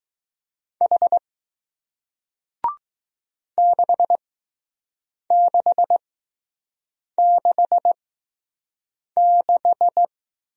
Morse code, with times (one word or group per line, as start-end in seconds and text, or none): 0.8-1.2 H
3.6-10.1 6666